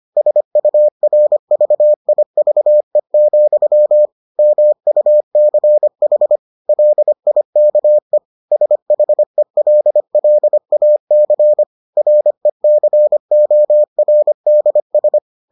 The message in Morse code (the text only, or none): SURVIVE, MUCH LIKE SHELLAC RECORDS